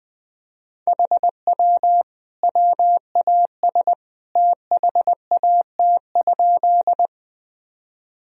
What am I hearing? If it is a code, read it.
Morse code, heard as a keyed tone, about 20 wpm, 700 Hz. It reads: HW WAS THAT?
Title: Sentences from Top 100 Words and Abbreviations 20wpm